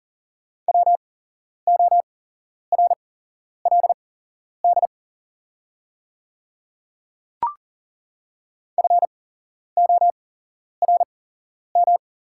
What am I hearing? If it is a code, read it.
Morse code, heard as a keyed tone, about 40 wpm, spaced as for 11 wpm, 700 Hz. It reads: WORLD FORM
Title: 8x Character Spacing - Top 200 Words - 40wpm